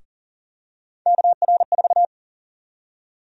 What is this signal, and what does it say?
Morse code, heard as a keyed tone, about 40 wpm, 700 Hz. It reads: KR4
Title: USA Call Sign Prefixes - Repeated 2x - 40wpm